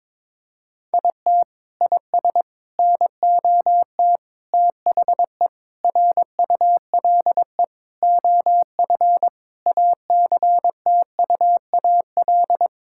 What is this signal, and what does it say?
Morse code, heard as a keyed tone, about 22 wpm, 700 Hz. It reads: IT IS NOT THE RULE OF ACTUAL
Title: Sentences from Top 1000 Words - Review Part 5 - 22wpm